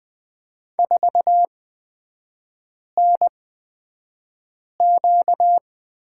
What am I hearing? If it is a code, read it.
Morse code, heard as a keyed tone, about 20 wpm, spaced as for 5 wpm, 700 Hz. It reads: 4NQ
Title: 236 Lesson 38e - Letter q - Three Characters 20wpm